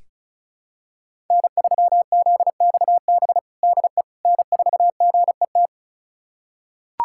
Morse code, heard as a keyed tone, about 34 wpm, 700 Hz. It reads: N3ZXB DE N4GET
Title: USA Callsign DE Callsign - 5 Characters - 35wpm